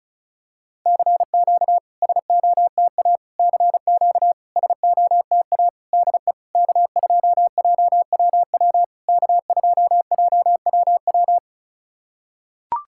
Morse code, heard as a keyed tone, about 34 wpm, 690 Hz. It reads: CQ SOTA CQ SOTA DE K2JWW K2JWW
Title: SOTA QSO Practice 35wpm